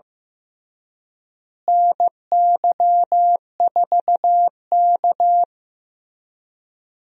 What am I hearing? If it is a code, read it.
Morse code, heard as a keyed tone, about 15 wpm, 700 Hz. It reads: NY4K
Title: USA Call Signs - 4 Characters - Repeated 3x - 15wpm